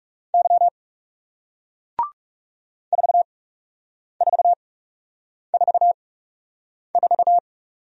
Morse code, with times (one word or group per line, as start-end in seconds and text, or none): 0.3-0.7 Y
2.9-7.4 4444